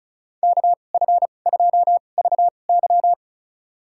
Morse code, heard as a keyed tone, about 35 wpm, spaced as for 25 wpm, 700 Hz. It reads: KF2VY
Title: USA Call Signs - 5 Characters - 2x Character Spacing - 35wpm